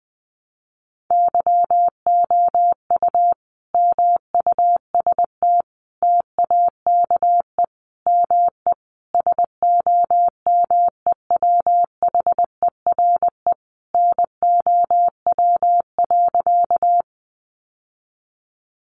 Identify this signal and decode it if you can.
Morse code, heard as a keyed tone, about 20 wpm, 700 Hz. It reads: YOU MUST TAKE ME SOMEWHERE NOW.
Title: Sentences from Top 1400 Words 20wpm